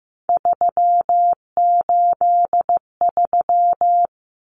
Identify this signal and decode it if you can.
Morse code, heard as a keyed tone, about 15 wpm, 700 Hz. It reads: 383
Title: Numbers - 3-Digit 15wpm